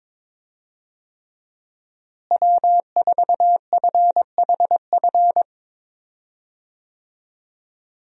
Morse code, heard as a keyed tone, about 22 wpm, 700 Hz. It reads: W4FHF
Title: USA Call Signs - 5 Characters - Repeated 3x - 22wpm